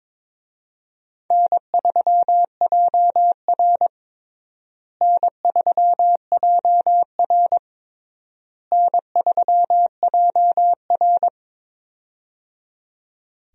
Morse code, heard as a keyed tone, about 22 wpm, 700 Hz. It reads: N3JR N3JR N3JR
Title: USA Call Signs - 4 Characters - Repeated 3x - 22wpm